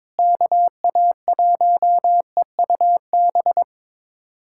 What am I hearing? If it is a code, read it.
Morse code, heard as a keyed tone, about 21 wpm, 700 Hz. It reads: KA1EUB
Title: USA Callsign DE Callsign 22wpm